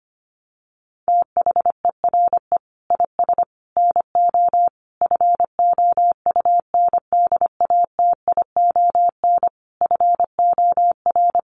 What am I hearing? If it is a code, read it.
Morse code, heard as a keyed tone, about 25 wpm, 700 Hz. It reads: THERE IS NO FOUNDATION FOR
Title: Sentences from Top 1200 Words 25wpm